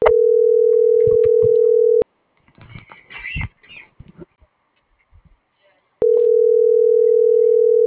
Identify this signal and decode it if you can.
Ringback tone (North America), 440 + 480 Hz.